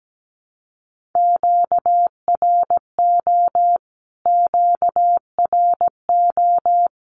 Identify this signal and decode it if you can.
Morse code, heard as a keyed tone, about 17 wpm, 700 Hz. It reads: QRO QRO
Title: Q-Codes Repeated 3x 17wpm